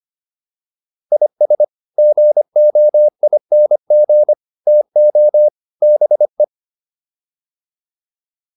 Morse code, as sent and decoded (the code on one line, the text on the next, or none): .. ... / --. --- .. -. --. / - --- / -... .
IS GOING TO BE